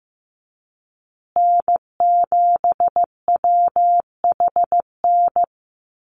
Morse code, heard as a keyed tone, about 15 wpm, 700 Hz. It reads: N7WHN
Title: USA Call Signs Repeated 2x 15wpm